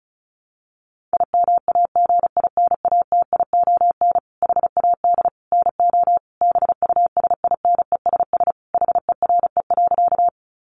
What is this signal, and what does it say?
Morse code, heard as a keyed tone, about 34 wpm, 700 Hz. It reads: IMAGINATION HAD NO BUSINESS HERE.
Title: Sentences from Top 1300 Words 35wpm